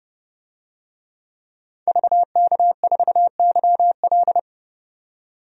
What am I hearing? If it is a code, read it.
Morse code, heard as a keyed tone, about 32 wpm, 700 Hz. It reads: VK4YL
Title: DXCC-100 Call Signs - 5 Characters - 30wpm